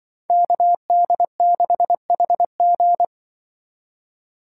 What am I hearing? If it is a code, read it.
Morse code, heard as a keyed tone, about 23 wpm, 700 Hz. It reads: KD6HG
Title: Speed Racing - USA 2x2 Call Signs 20wpm